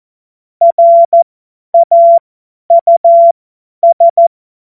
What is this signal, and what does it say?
Morse code, heard as a keyed tone, about 14 wpm, spaced as for 10 wpm, 680 Hz. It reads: RAUS
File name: Morsekurs / Witze / Witz 24 Juni 2025 50 F70